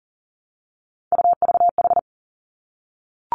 Morse code, heard as a keyed tone, about 40 wpm, 700 Hz. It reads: UVH
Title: Three Challenging Letter-Number Combinations 40wpm